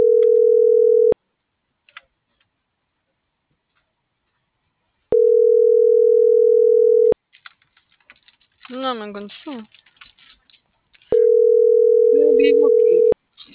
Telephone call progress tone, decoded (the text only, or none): ringback tone